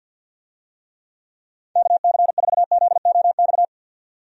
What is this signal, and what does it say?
Morse code, heard as a keyed tone, about 52 wpm, 700 Hz. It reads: KC4ZOX
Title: USA Call Signs - 6 Characters - 50wpm